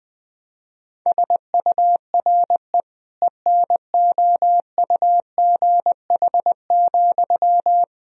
Morse code, heard as a keyed tone, about 20 wpm, 700 Hz. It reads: SURE ENOUGH,
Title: Sentences from Top 1000 Words - Review Part 2 - 20wpm